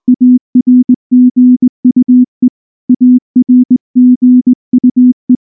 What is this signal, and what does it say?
Morse code, heard as a keyed tone, about 19 wpm, 250 Hz. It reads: ARGUE ARGUE
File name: morse / morse-training-20WPM-6